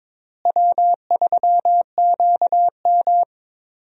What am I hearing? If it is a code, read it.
Morse code, heard as a keyed tone, about 22 wpm, 700 Hz. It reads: W3QM
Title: USA Call Signs - 4 Characters - Repeated 3x - 22wpm